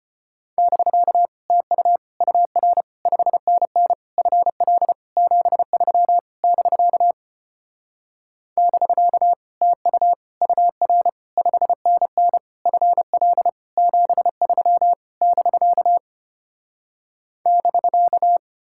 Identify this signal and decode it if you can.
Morse code, heard as a keyed tone, about 33 wpm, 700 Hz. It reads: <BK> TU UR 5NN FL 73 <BK> <BK> TU UR 5NN FL 73 <BK> <BK>